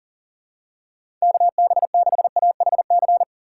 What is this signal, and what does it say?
Morse code, heard as a keyed tone, about 40 wpm, 700 Hz. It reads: KB6AHC
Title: USA Call Signs - 6 Characters - 3x Character Spacing - 40wpm